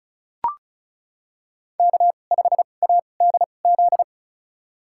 Morse code, heard as a keyed tone, about 36 wpm, spaced as for 25 wpm, 700 Hz. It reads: K5ADZ
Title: USA Call Signs - 2x Character Spacing - 35wpm